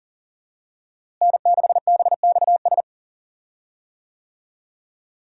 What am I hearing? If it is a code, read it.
Morse code, heard as a keyed tone, about 39 wpm, 700 Hz. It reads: N6BXS
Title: USA Call Signs - 5 Characters - Repeated 3x - 40wpm